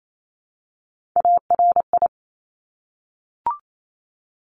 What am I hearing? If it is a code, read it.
Morse code, heard as a keyed tone, about 28 wpm, 700 Hz. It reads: ARI
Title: Top Three Letter Combinations in Words 28wpm